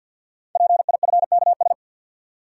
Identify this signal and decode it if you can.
Morse code, heard as a keyed tone, about 46 wpm, 700 Hz. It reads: PIFKS
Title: Code Groups - 5 Letters - 50wpm